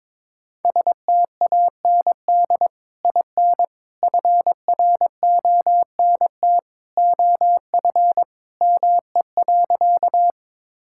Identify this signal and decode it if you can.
Morse code, heard as a keyed tone, about 22 wpm, 700 Hz. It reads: STAND IN FRONT OF ME.
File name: practice / 22wpm / Sentences with 2-Word Phrases - 22wpm